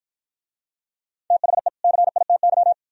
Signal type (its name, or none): Morse code